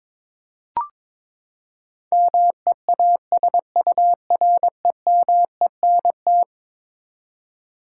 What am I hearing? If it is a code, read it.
Morse code, heard as a keyed tone, about 23 wpm, 700 Hz. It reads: MEASUREMENT
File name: practice / 22wpm / Eleven Letter Words - 22wpm